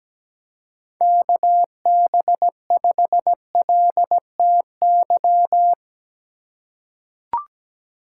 Morse code, heard as a keyed tone, about 17 wpm, 700 Hz. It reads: KB5LTY
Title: USA Call Signs - 4x Character Spacing - 17wpm